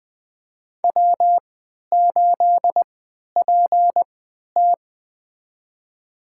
Morse code, heard as a keyed tone, about 21 wpm, spaced as for 11 wpm, 700 Hz. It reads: W8PT
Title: USA Call Signs - 4 Characters - 3x Character Spacing - 20wpm